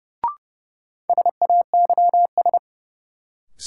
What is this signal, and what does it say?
Morse code, heard as a keyed tone, about 30 wpm, 700 Hz. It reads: SAYS
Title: Sets of 2 Words - New Word Drill - 30wpm